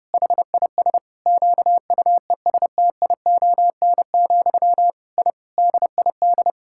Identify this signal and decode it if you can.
Morse code, heard as a keyed tone, about 32 wpm, 700 Hz. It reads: HIS QUESTION, I DID
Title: Sentences from Top 1000 Words - Review Part 6 - 30wpm